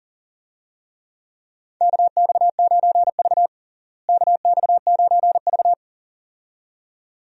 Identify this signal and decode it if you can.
Morse code, heard as a keyed tone, about 40 wpm, 700 Hz. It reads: KX9V KX9V
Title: USA Call Signs - 4 Characters - Repeated 2x - 40wpm